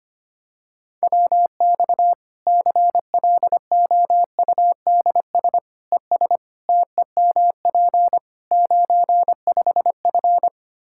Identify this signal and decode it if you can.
Morse code, heard as a keyed tone, about 25 wpm, 700 Hz. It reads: WX CLOUDS ES TEMP 95F
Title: Practice QSO - Exchange 2 - Info - 25wpm